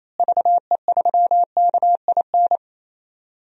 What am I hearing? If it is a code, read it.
Morse code, heard as a keyed tone, about 28 wpm, 700 Hz. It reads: VE3KIN
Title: DXCC-100 Call Signs - 6 Characters - Repeated 3x - 28wpm